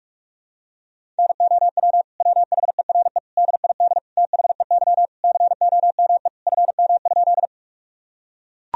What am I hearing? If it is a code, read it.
Morse code, heard as a keyed tone, about 41 wpm, 700 Hz. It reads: NOW WHERE DID THEY COME FM?
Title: Sentences from Top 200 Words and Abbreviations 45wpm